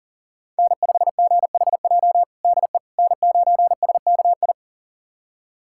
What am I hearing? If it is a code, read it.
Morse code, heard as a keyed tone, about 40 wpm, 700 Hz. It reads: N5GHJ DE N9SKI
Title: USA Callsign DE Callsign - 5 Characters - 40wpm